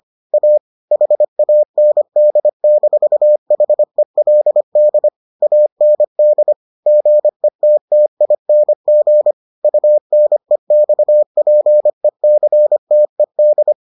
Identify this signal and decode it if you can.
Morse code, heard as a keyed tone, about 25 wpm, 600 Hz. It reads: A HAND-HELD AND GETTING UNEXPECTED